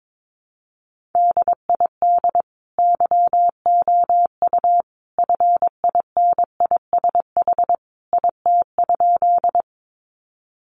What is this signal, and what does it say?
Morse code, heard as a keyed tone, about 22 wpm, 700 Hz. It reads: DID YOU FINISH IT?
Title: Sentences from Top 1200 Words 22wpm